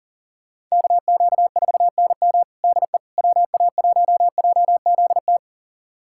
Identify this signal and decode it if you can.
Morse code, heard as a keyed tone, about 40 wpm, 700 Hz. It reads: KQ4NM DE WA1JZT
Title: USA Callsign DE Callsign 40wpm